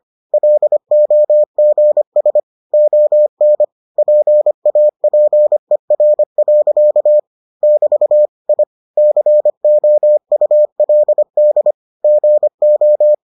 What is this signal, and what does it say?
Morse code, heard as a keyed tone, about 25 wpm, 600 Hz. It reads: LOGS ON PAPER. <BT> I COULD GO